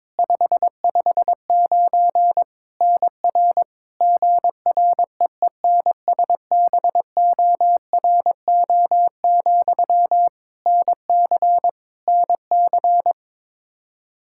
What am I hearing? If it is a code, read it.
Morse code, heard as a keyed tone, about 22 wpm, 700 Hz. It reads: 559 NR GREENSBORO, NC NC
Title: K1USN SST QSO Practice 22wpm